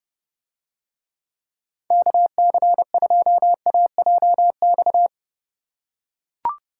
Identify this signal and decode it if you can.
Morse code, heard as a keyed tone, about 30 wpm, 700 Hz. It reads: KC2AJX